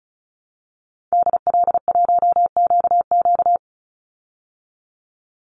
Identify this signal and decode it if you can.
Morse code, heard as a keyed tone, about 34 wpm, 700 Hz. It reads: DL1QQ
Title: Call Signs Repeated 3x - 35wpm